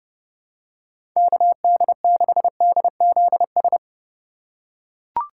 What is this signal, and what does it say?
Morse code, heard as a keyed tone, about 30 wpm, 700 Hz. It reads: KD6DZS
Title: USA Call Signs - 6 Characters - 4x Character Spacing - 30wpm